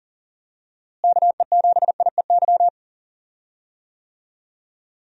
Morse code, heard as a keyed tone, about 40 wpm, 700 Hz. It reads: KE7IEY
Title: USA Call Signs - 6 Characters - Repeated 3x - 40wpm